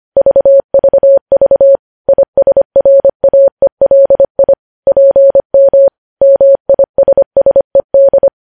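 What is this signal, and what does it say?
Morse code, heard as a keyed tone, about 25 wpm, 550 Hz. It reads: VVV ISRAELI PM MISSED
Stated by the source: Morse code news headlines sent at 25 words per minute.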